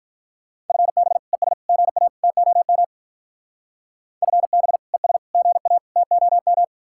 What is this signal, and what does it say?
Morse code, heard as a keyed tone, about 41 wpm, 700 Hz. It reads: FB ES GA TOM FB ES GA TOM
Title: Speed-Racing Common QSO Phrases - 35wpm